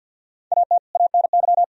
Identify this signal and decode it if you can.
Morse code, heard as a keyed tone, about 50 wpm, 700 Hz. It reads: AT ANY
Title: Sentences from Top 1000 Words - Review Part 4 - 50wpm